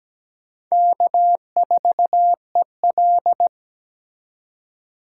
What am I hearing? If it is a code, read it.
Morse code, heard as a keyed tone, about 20 wpm, 700 Hz. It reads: K4EL
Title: Speed Racing - USA 1x2 Call Signs 17wpm